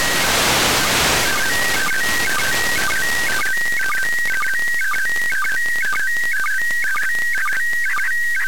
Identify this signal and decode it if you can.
SSTV (slow-scan television), PD 120.
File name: ariss-sstv / partial-ariss- -1440